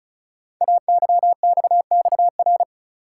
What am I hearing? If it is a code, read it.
Morse code, heard as a keyed tone, about 36 wpm, 690 Hz. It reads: AYXXR
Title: Code Groups - 5 Letters - 35wpm